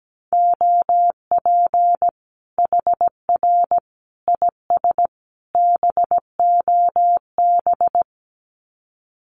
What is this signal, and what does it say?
Morse code, heard as a keyed tone, about 17 wpm, 700 Hz. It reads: OP HR IS BOB
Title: Common QSO Phrases 17wpm